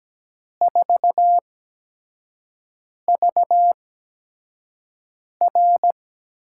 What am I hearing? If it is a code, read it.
Morse code, heard as a keyed tone, about 17 wpm, spaced as for 5 wpm, 700 Hz. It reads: 4VR
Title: USA Call Signs - 6 Characters - 8x Character Spacing - 17wpm